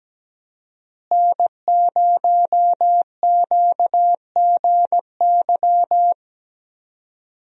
Morse code, heard as a keyed tone, about 17 wpm, 700 Hz. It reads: N0QGY
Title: USA Call Signs - 5 Characters - Repeated 2x - 17wpm